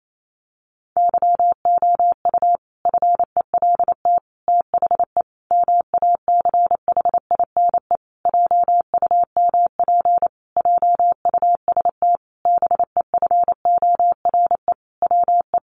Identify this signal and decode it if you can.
Morse code, heard as a keyed tone, about 28 wpm, 700 Hz. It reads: YOU FELT THE MACHINE JUMP JUST BEFORE WE
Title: A Sound Of Thunder 28wpm